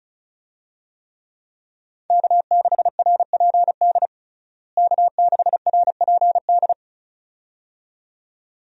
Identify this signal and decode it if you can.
Morse code, heard as a keyed tone, about 34 wpm, 700 Hz. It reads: K6RPD K6RPD
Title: USA Call Signs - 5 Characters - Repeated 2x - 35wpm